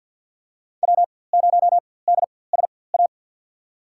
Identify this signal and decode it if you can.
Morse code, heard as a keyed tone, about 47 wpm, spaced as for 23 wpm, 700 Hz. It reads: W0DSA
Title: USA Call Signs - 5 Characters - 4x Character Spacing - 50wpm